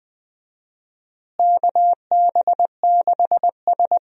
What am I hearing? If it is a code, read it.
Morse code, heard as a keyed tone, about 21 wpm, 700 Hz. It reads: KB6S
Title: USA Call Signs - 4 Characters - Repeated 3x - 20wpm